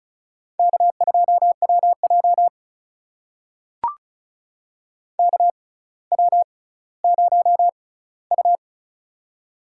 Morse code, heard as a keyed tone, about 36 wpm, spaced as for 18 wpm, 700 Hz. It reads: K2WJ KW0U
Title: USA Call Signs - 4 Characters - 6x Character Spacing - 35wpm